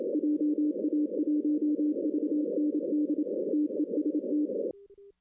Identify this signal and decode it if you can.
Morse code, heard as a keyed tone, about 28 wpm, 300 Hz. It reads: JA1VND TEST
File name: 2016 arrl dx cw / 2016-ARRL-DX-CW-K5TR- - 0-26